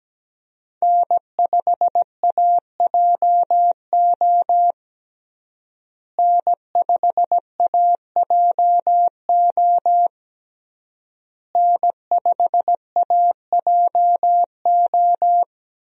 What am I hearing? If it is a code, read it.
Morse code, heard as a keyed tone, about 17 wpm, 700 Hz. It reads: N5AJO N5AJO N5AJO
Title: USA Call Signs - 5 Characters - Repeated 3x - 17wpm